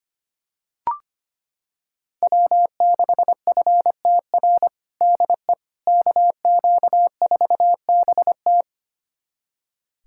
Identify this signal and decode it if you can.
Morse code, heard as a keyed tone, about 25 wpm, 700 Hz. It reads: W6FTR DE KQ4BT
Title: USA Callsign DE Callsign - 5 Characters - 25wpm